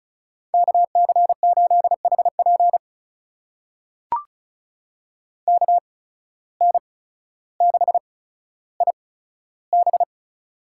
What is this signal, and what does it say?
Morse code, heard as a keyed tone, about 36 wpm, spaced as for 15 wpm, 700 Hz. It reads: KC8HP KN6IB
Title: USA Call Signs - 5 Characters - 8x Character Spacing - 35wpm